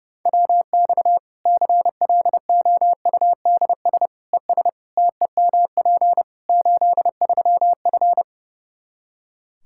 Morse code, heard as a keyed tone, about 32 wpm, 700 Hz. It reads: WX CLOUDS ES TEMP 83F
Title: K1USN SST QSO Practice 30wpm